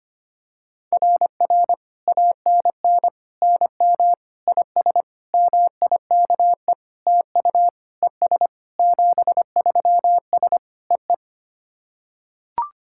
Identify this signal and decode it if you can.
Morse code, heard as a keyed tone, about 25 wpm, 690 Hz. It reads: RR ANN NM IS MIKE TU ES 73S EE
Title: SOTA QSO Practice 25wpm